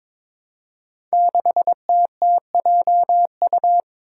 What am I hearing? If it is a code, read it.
Morse code, heard as a keyed tone, about 22 wpm, 700 Hz. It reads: 6TTJU
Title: Five Challenging Letter-Number Combinations 22wpm